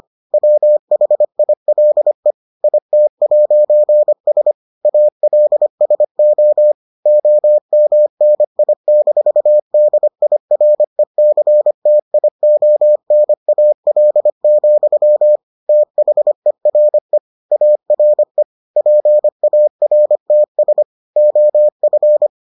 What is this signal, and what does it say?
Morse code, heard as a keyed tone, about 25 wpm, 600 Hz. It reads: WHILE IT'S ALSO OMNI-DIRECTIONAL, THERE ARE PARTS OF